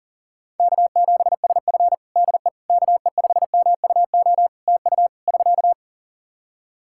Morse code, heard as a keyed tone, about 40 wpm, 700 Hz. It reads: K7SF DE KE5MUO TU <SK>
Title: K1USN SST QSO Practice 40wpm